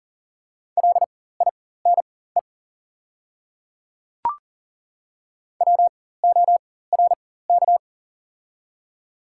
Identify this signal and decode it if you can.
Morse code, heard as a keyed tone, about 40 wpm, spaced as for 19 wpm, 700 Hz. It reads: LINE WORK
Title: 4x Character Spacing - Top 200 Words - 40wpm